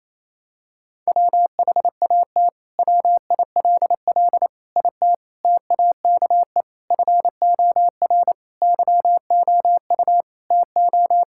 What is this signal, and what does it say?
Morse code, heard as a keyed tone, about 28 wpm, 700 Hz. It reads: WHAT WILL IT TAKE FOR YOU TO
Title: Sentences from Top 1600 Words 28wpm